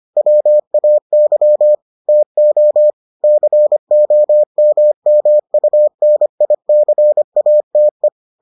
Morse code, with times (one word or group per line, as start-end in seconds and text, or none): 0.2-1.8 WAY
2.1-2.9 TO
3.2-8.1 COMMUNICATE